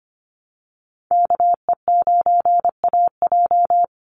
Morse code, heard as a keyed tone, about 25 wpm, 700 Hz. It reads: KE9AJ
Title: NA SOTA Chaser Callsigns 25wpm